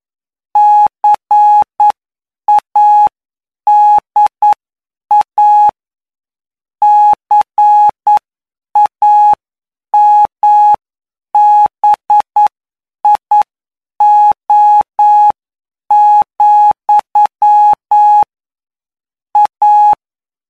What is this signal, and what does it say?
Morse code, heard as a keyed tone, about 9 wpm, spaced as for 7 wpm, 840 Hz. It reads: CADA CAMBIO, A
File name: cw / py2lcd / 20-PY2LCD-Li��o 29